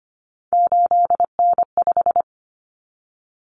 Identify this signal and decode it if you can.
Morse code, heard as a keyed tone, about 25 wpm, 700 Hz. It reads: 8N5
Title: Common Cut Numbers - 3-Digit 25wpm